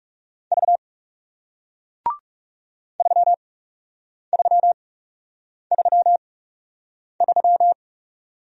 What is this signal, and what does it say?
Morse code, heard as a keyed tone, about 38 wpm, spaced as for 8 wpm, 700 Hz. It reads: V 3333